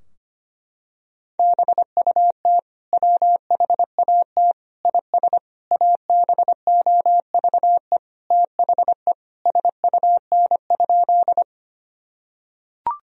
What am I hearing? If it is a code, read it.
Morse code, heard as a keyed tone, about 25 wpm, 700 Hz. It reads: BUT WHAT IS ABOVE THE SUN?
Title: Sentences from Top 1000 Words - Review Part 2 - 25wpm